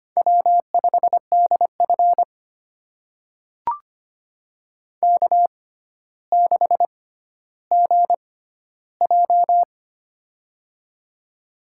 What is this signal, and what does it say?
Morse code, heard as a keyed tone, about 25 wpm, spaced as for 13 wpm, 700 Hz. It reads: W5DF K6GJ
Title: USA Call Signs - 4 Characters - 6x Character Spacing - 25wpm